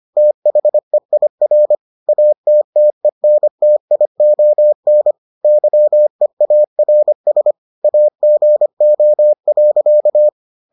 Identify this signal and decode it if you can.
Morse code, heard as a keyed tone, about 25 wpm, 600 Hz. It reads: THEIR ATTENTION YEARS AGO.